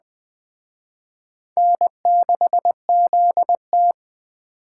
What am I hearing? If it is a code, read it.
Morse code, heard as a keyed tone, about 20 wpm, 700 Hz. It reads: N6ZT